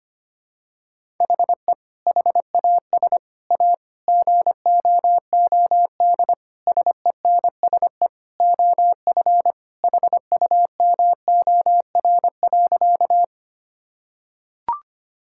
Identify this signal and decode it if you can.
Morse code, heard as a keyed tone, about 25 wpm, 700 Hz. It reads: HE HAS A GOOD SENSE OF HUMOR.
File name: practice / 25wpm / Sentences with 2-Word Phrases - 25wpm